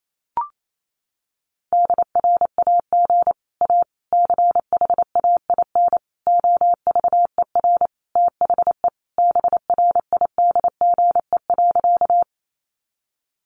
Morse code, heard as a keyed tone, about 28 wpm, 700 Hz. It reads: DRAG A CHAIN OVER THE BRIDGE.
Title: Sentences from Top 1500 Words 28wpm